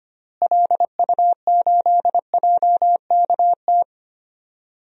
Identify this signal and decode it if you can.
Morse code, heard as a keyed tone, about 25 wpm, 700 Hz. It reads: LU8JKT
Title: DXCC-100 Call Signs - 6 Characters - Repeated 3x - 25wpm